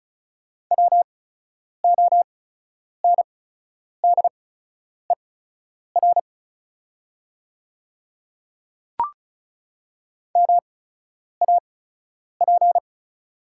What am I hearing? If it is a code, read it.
Morse code, heard as a keyed tone, about 36 wpm, spaced as for 10 wpm, 700 Hz. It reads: WONDER MAP
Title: 8x Character Spacing - Top 500 Words - 35wpm